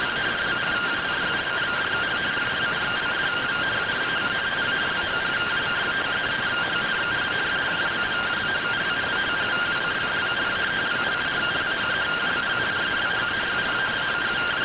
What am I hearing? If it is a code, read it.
Two-tone teleprinter signal, about 170 Hz shift.